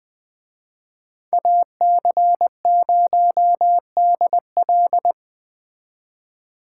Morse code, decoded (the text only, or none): AC0DL